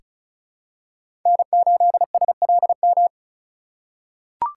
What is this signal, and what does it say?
Morse code, heard as a keyed tone, about 36 wpm, 700 Hz. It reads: N8SLM
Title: USA Call Signs - 3x Character Spacing - 35wpm